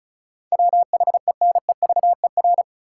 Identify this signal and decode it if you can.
Morse code, heard as a keyed tone, about 36 wpm, 700 Hz. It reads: WHENEVER
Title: Sentences from Top 1200 Words - New Word Drill Alt - 35wpm